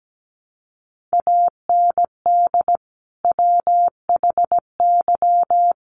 Morse code, heard as a keyed tone, about 17 wpm, 700 Hz. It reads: AND WHY
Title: Sentences from Top 1300 Words 17wpm